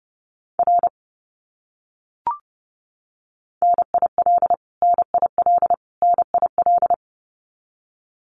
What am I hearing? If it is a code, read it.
Morse code, heard as a keyed tone, about 30 wpm, 700 Hz. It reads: R NIL NIL NIL
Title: Abbreviations Repeated 3x - 30wpm